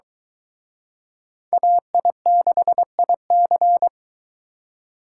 Morse code, heard as a keyed tone, about 24 wpm, 700 Hz. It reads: AI6IC